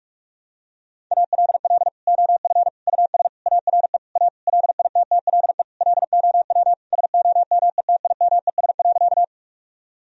Morse code, heard as a keyed tone, about 46 wpm, 700 Hz. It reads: ALL OF US ARE A LITTLE LOW SOMETIMES.
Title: Sentences from Top 1000 Words - Review Part 3 - 45wpm